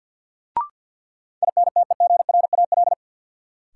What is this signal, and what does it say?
Morse code, heard as a keyed tone, about 47 wpm, 700 Hz. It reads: INTEGRAL
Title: Sets of 5 Words - New Word Drill - 50wpm